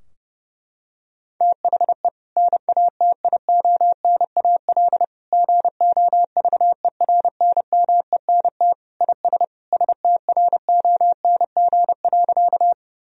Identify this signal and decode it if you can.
Morse code, heard as a keyed tone, about 30 wpm, 700 Hz. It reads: THE NATIONAL GOVERNMENT IS STRONG.
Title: Sentences from Top 1000 Words - Review Part 4 - 30wpm